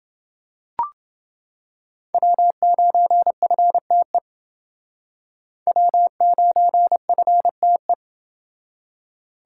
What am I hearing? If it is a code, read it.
Morse code, heard as a keyed tone, about 29 wpm, 700 Hz. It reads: W9FTE W9FTE